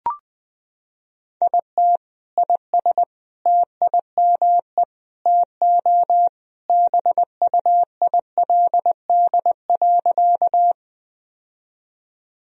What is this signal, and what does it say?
Morse code, heard as a keyed tone, about 20 wpm, 700 Hz. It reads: IT IS TIME TO BUILD.
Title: Sentences from Top 1000 Words - Review Part 1 - 20wpm